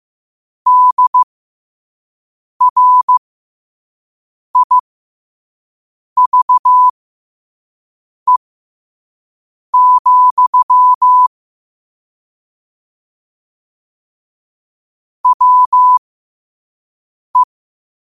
Morse code, heard as a keyed tone, about 15 wpm, spaced as for 5 wpm, 1000 Hz. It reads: DRIVE, WE
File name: qotdfiles / 5wpm